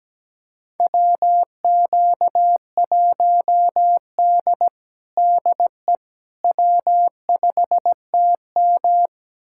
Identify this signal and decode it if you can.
Morse code, heard as a keyed tone, about 17 wpm, 700 Hz. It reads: WQ1D DE W5TM
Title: USA Callsign DE Callsign - 4 Characters - 17wpm